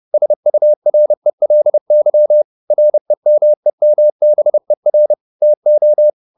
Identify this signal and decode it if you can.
Morse code, heard as a keyed tone, about 30 wpm, 600 Hz. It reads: SURELY REMEMBER TO
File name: the old man and the sea / the old man and the sea 1 30wpm